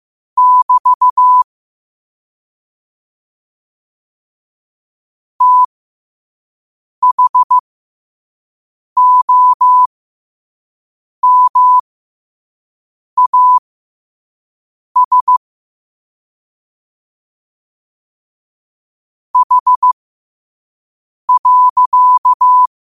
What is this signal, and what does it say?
Morse code, heard as a keyed tone, about 15 wpm, spaced as for 5 wpm, 1000 Hz. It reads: <BT> THOMAS H.
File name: qotdfiles / 5wpm